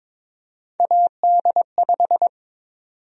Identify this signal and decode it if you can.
Morse code, heard as a keyed tone, about 22 wpm, 700 Hz. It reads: AD5
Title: USA Call Sign Prefixes - 22wpm